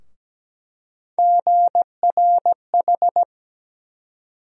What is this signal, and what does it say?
Morse code, heard as a keyed tone, about 17 wpm, 700 Hz. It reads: GRH
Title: USA Call Sign Postfixes - 3 Letters 17wpm